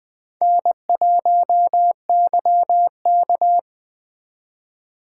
Morse code, heard as a keyed tone, about 20 wpm, 700 Hz. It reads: N1YK
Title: USA Call Signs - 4 Characters - 2x Character Spacing - 20wpm